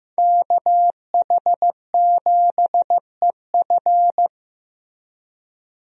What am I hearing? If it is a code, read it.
Morse code, heard as a keyed tone, about 15 wpm, 700 Hz. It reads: KH7EF
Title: USA Call Signs - 5 Characters - Repeated 2x - 15wpm